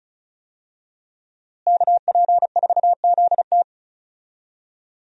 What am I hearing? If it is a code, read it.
Morse code, heard as a keyed tone, about 36 wpm, 700 Hz. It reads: KP4ZT
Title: USA Call Signs - 5 Characters - Repeated 2x - 35wpm